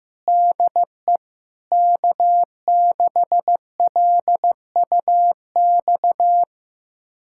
Morse code, heard as a keyed tone, about 15 wpm, 700 Hz. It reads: DE K6LUX
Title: USA Callsign DE Callsign 15wpm